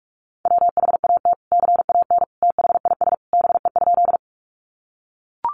Morse code, heard as a keyed tone, about 44 wpm, 700 Hz. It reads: WHAT CAN THIS BE?
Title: Sentences from Top 100 Words 45wpm